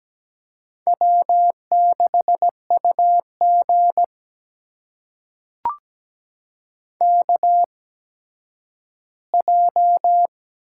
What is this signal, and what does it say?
Morse code, heard as a keyed tone, about 17 wpm, spaced as for 10 wpm, 700 Hz. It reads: W6UG KJ
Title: USA Call Signs - 8x Character Spacing - 17wpm